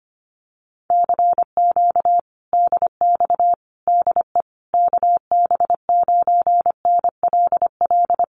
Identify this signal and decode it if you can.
Morse code, heard as a keyed tone, about 25 wpm, 700 Hz. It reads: CQ DX DE KB9NLL
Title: Common QSO Phrases Encore 25wpm